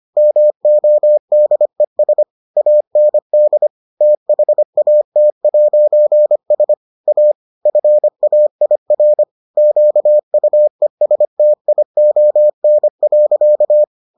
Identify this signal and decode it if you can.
Morse code, heard as a keyed tone, about 25 wpm, 600 Hz. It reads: MODES AND THAT'S A FAIR QUESTION.